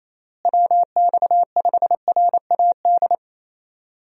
Morse code, heard as a keyed tone, about 28 wpm, 700 Hz. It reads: WX5RAD
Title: USA Call Signs - 6 Characters - Repeated 3x - 28wpm